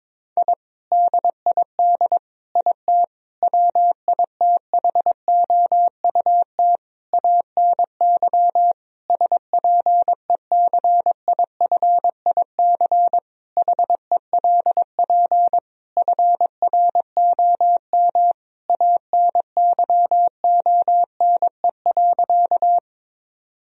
Morse code, heard as a keyed tone, about 22 wpm, 700 Hz. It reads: I DID IT WITHOUT ANY SPECIFIC HELP FROM ANYONE.
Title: Sentences from Top 1000 Words - Review Part 3 - 22wpm